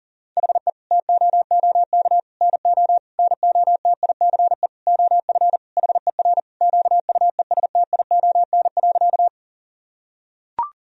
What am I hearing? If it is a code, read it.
Morse code, heard as a keyed tone, about 37 wpm, 700 Hz. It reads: HE TOOK NO NOTICE OF HER QUESTION.
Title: Sentences from Top 1000 Words - Review Part 2 - 40wpm